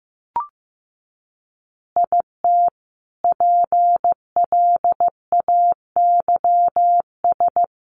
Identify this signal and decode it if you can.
Morse code, heard as a keyed tone, about 15 wpm, 700 Hz. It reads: IT PLAYS
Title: Additional Easy Phrases 15wpm